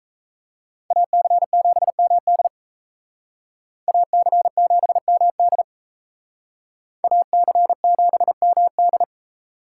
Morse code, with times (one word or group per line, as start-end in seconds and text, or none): 0.9-2.5 AC7MD
3.9-5.6 AC7MD
7.0-9.0 AC7MD